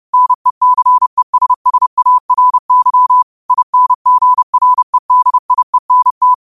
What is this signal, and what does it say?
Morse code, heard as a keyed tone, about 30 wpm, 1000 Hz. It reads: NECESSARY INGREDIENT
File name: qotdfiles / 30wpm